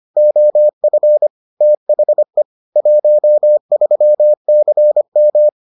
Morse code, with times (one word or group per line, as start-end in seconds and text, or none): 0.2-1.3 OF
1.6-2.4 THE
2.8-5.5 13CM